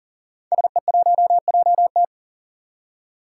Morse code, heard as a keyed tone, about 40 wpm, 700 Hz. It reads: SE1JT
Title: Five Challenging Letter-Number Combinations 40wpm